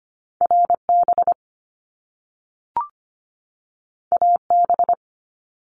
Morse code, heard as a keyed tone, about 25 wpm, 700 Hz. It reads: RB AB
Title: Top Two Letter Combinations in Words 25wpm